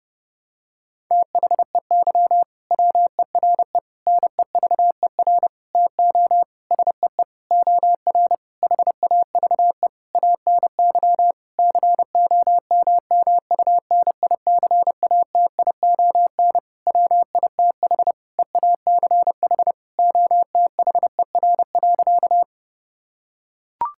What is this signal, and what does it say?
Morse code, heard as a keyed tone, about 30 wpm, 700 Hz. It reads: THEY WERE NEVER TO SEE OR HAVE ANY COMMUNICATION WITH EACH OTHER.
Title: Sentences from Top 1000 Words - Review Part 6 - 30wpm